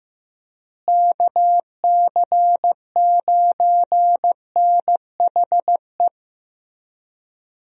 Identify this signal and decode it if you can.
Morse code, heard as a keyed tone, about 15 wpm, 700 Hz. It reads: KC9NHE
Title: USA Call Signs - 6 Characters - Repeated 3x - 15wpm